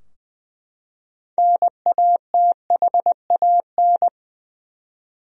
Morse code, heard as a keyed tone, about 20 wpm, 700 Hz. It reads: NATHAN
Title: USA Common Names - 20wpm